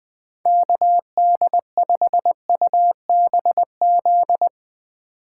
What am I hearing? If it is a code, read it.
Morse code, heard as a keyed tone, about 20 wpm, 700 Hz. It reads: KD5UBZ
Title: USA Call Signs - 3x Character Spacing - 20wpm